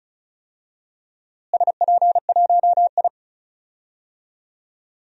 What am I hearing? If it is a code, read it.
Morse code, heard as a keyed tone, about 34 wpm, 700 Hz. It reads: SP1I
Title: DXCC-100 Call Signs - 4 Characters - Repeated 2x - 35wpm